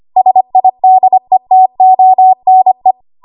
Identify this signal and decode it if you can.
Morse code, heard as a keyed tone, about 25 wpm, 750 Hz. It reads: SIDETONE
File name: Archive / 25 WPM / 25WPM